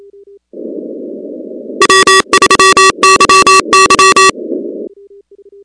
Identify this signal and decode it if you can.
Morse code, heard as a keyed tone, about 28 wpm, 1150 Hz, with noise. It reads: W3YY
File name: contests / 2019-cw-wpx-kt5j-k5pi- -2330 0-16